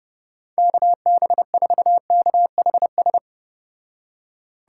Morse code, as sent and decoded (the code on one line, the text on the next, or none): -.- -... ....- -.- .... ...
KB4KHS